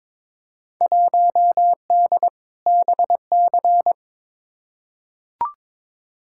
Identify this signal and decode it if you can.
Morse code, heard as a keyed tone, about 21 wpm, 700 Hz. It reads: 1D BC
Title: Field Day Exchanges - 22wpm